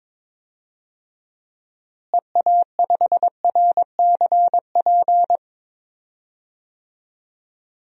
Morse code, heard as a keyed tone, about 22 wpm, 700 Hz. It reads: EA5RCP
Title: DXCC-100 Call Signs - 6 Characters - Repeated 3x - 22wpm